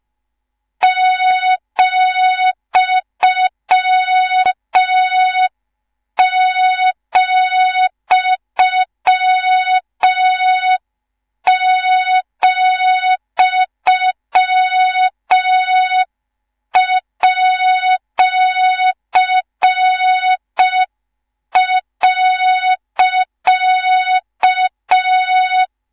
Morse code, heard as a keyed tone, about 5 wpm, 3700 Hz, with noise. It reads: ,,,@.